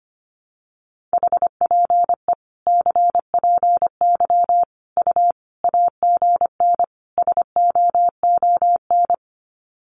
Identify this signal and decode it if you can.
Morse code, heard as a keyed tone, about 25 wpm, 700 Hz. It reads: HPE CPY U AGN SOON
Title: Common QSO Phrases 25wpm